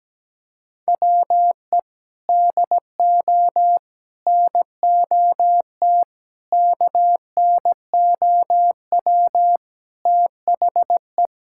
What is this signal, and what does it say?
Morse code, heard as a keyed tone, about 17 wpm, 700 Hz. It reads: WE DO NOT KNOW THE
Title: Sentences from Top 1000 Words - Review Part 6 - 17wpm